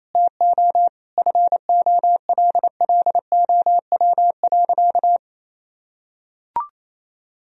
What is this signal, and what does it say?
Morse code, heard as a keyed tone, about 28 wpm, 700 Hz. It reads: TO FOLLOW.
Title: Sentences from Top 1000 Words - Review Part 2 - 28wpm